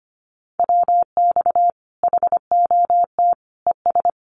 Morse code, heard as a keyed tone, about 25 wpm, 700 Hz. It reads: WX HOT ES
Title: Common QSO Phrases 25wpm